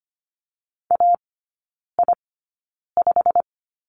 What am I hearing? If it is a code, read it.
Morse code, heard as a keyed tone, about 25 wpm, spaced as for 9 wpm, 690 Hz. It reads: AI5
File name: Lesson 15e - Number 5 - Three Characters 25wpm